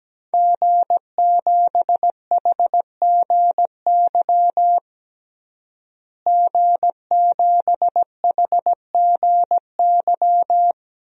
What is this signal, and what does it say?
Morse code, heard as a keyed tone, about 17 wpm, 700 Hz. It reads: G7HGY G7HGY
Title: DXCC-100 Call Signs - 5 Characters - Repeated 2x - 17wpm